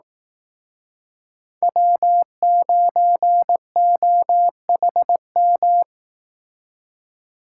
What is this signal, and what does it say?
Morse code, heard as a keyed tone, about 18 wpm, 700 Hz. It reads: W9OHM
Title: Speed Racing - USA 1x3 Call Signs 15wpm